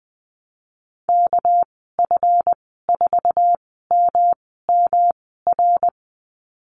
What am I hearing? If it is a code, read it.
Morse code, heard as a keyed tone, about 20 wpm, spaced as for 15 wpm, 700 Hz. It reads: KF4MMR
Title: 207 Lesson 33g - Letter j - Call Signs 20wpm